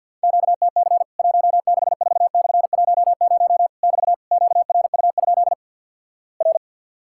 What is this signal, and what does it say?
Morse code, heard as a keyed tone, about 50 wpm, 700 Hz. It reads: QTC 164/10 <BT> QRU? R
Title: Worked All Europe DX Contest - QTC Exchange with Repeats - 50wpm